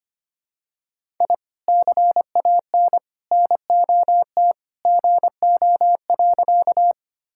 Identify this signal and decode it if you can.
Morse code, heard as a keyed tone, about 25 wpm, 700 Hz. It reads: I CAN NOT GO.
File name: practice / 25wpm / Sentences with 2-Word Phrases - 25wpm